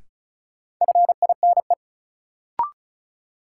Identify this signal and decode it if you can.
Morse code, heard as a keyed tone, about 36 wpm, 700 Hz. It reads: FINE
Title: Sentences from Top 500 Words - New Word Drill - 35wpm